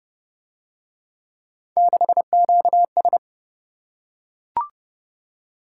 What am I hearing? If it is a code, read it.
Morse code, heard as a keyed tone, about 29 wpm, 700 Hz. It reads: 6QS